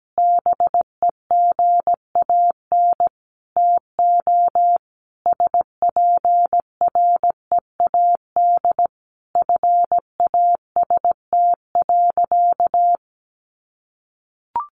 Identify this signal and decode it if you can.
Morse code, heard as a keyed tone, about 17 wpm, 700 Hz. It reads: BEGAN TO SPREAD FAST.
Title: Sentences from Top 1200 Words 17wpm